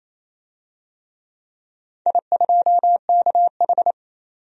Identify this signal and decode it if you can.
Morse code, heard as a keyed tone, about 27 wpm, 690 Hz. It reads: I2KH
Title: DXCC-100 Call Signs - 4 Characters - 28wpm